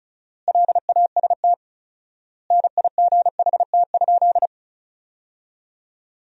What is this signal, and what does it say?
Morse code, heard as a keyed tone, about 34 wpm, 700 Hz. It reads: LAST NIGHT?
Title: 4x Word-Spacing - Sentences from Top 300 Words - 35wpm